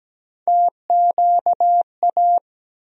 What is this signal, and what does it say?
Morse code, heard as a keyed tone, about 17 wpm, 700 Hz. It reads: TQA
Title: USA Call Sign Postfixes - 3 Letters 17wpm